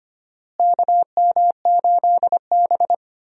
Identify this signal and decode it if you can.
Morse code, heard as a keyed tone, about 25 wpm, 700 Hz. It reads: KM8B
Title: USA Call Signs - 4 Characters - Repeated 2x - 25wpm